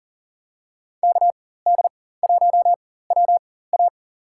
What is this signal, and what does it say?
Morse code, heard as a keyed tone, about 40 wpm, spaced as for 19 wpm, 700 Hz. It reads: KD1WA
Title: USA Call Signs - 4x Character Spacing - 40wpm